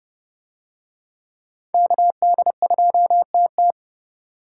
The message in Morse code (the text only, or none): KD2TT